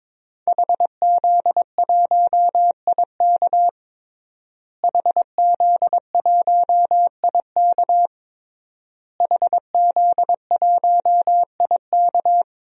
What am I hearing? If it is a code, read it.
Morse code, heard as a keyed tone, about 22 wpm, 700 Hz. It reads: HZ1IK HZ1IK HZ1IK
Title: DXCC-100 Call Signs - 5 Characters - Repeated 3x - 22wpm